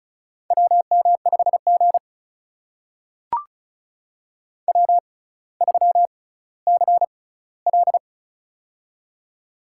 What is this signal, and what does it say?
Morse code, heard as a keyed tone, about 36 wpm, spaced as for 18 wpm, 700 Hz. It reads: WM5G W3CL
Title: USA Call Signs - 4 Characters - 6x Character Spacing - 35wpm